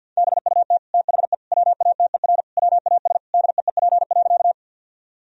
Morse code, heard as a keyed tone, about 50 wpm, 700 Hz. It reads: BUT THE WATER WAS DEEP.
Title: Sentences from Top 1000 Words - Review Part 3 - 50wpm